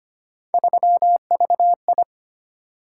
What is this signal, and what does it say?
Morse code, heard as a keyed tone, about 25 wpm, 700 Hz. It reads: 3VI
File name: Three Letter-Number Combinations - 25wpm